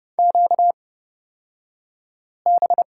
Morse code, heard as a keyed tone, about 30 wpm, spaced as for 5 wpm, 700 Hz. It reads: QB
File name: Lesson 38d - Letter q - Two Characters 30wpm